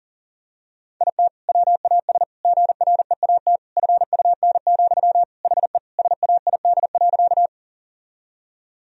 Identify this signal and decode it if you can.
Morse code, heard as a keyed tone, about 37 wpm, 700 Hz. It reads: IT WAS GREAT FUN, HE SAID.
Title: Sentences from Top 1000 Words - Review Part 4 - 40wpm